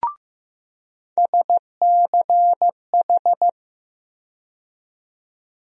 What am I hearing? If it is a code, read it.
Morse code, heard as a keyed tone, about 15 wpm, 700 Hz. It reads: SCH
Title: USA Call Sign Postfixes - 3 Letters 15wpm